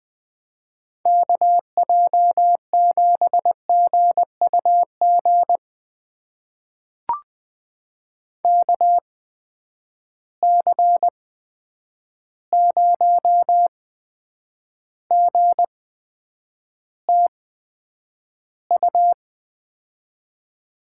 Morse code, heard as a keyed tone, about 21 wpm, spaced as for 9 wpm, 700 Hz. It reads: KJ7GUG KC0GTU
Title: USA Call Signs - 6 Characters - 8x Character Spacing - 20wpm